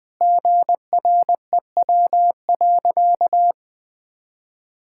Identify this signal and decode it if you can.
Morse code, heard as a keyed tone, about 20 wpm, 700 Hz. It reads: GREW.
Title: Sentences with 8-Word Phrases - 20wpm